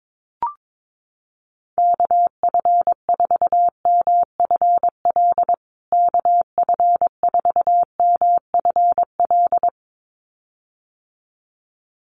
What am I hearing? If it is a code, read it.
Morse code, heard as a keyed tone, about 22 wpm, 700 Hz. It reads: KF4MFL KF4MFL
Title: USA Call Signs Repeated 2x 22wpm